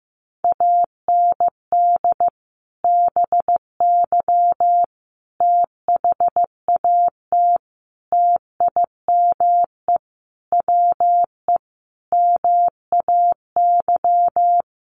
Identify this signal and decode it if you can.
Morse code, heard as a keyed tone, about 15 wpm, 700 Hz. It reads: AND BY THAT TIME WE MAY
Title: Sentences from Top 1100 Words 15wpm